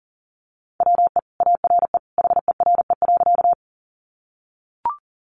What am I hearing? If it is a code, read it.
Morse code, heard as a keyed tone, about 40 wpm, 700 Hz. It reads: WE ARE HERE.
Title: Sentences from Top 200 Words 40wpm